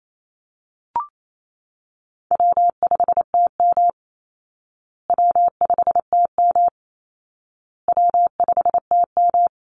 Morse code, heard as a keyed tone, about 28 wpm, 700 Hz. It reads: W5TM W5TM W5TM
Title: Call Signs Repeated 2x - 28wpm